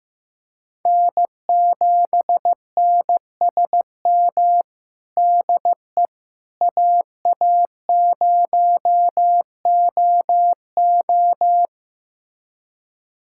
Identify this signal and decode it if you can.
Morse code, heard as a keyed tone, about 16 wpm, 700 Hz. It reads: N7NSM DE AA0OO
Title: USA Callsign DE Callsign - 5 Characters - 15wpm